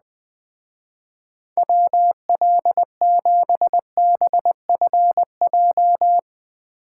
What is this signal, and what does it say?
Morse code, heard as a keyed tone, about 20 wpm, 700 Hz. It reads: WL7BFJ